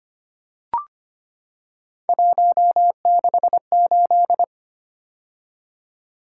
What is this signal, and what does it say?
Morse code, heard as a keyed tone, about 25 wpm, 700 Hz. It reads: 168
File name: Three Letter-Number Combinations - 25wpm